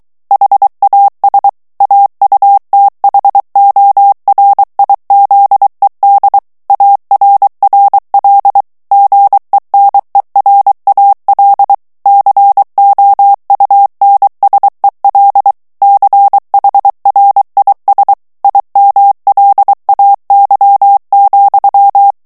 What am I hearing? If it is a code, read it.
Morse code, heard as a keyed tone, about 23 wpm, 800 Hz. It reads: HAS AUTHORIZED ARRL GENERAL COUNSEL CHRIS IMLAY,
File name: ANGLAIS texte clair 3 vitesse 22 mots minute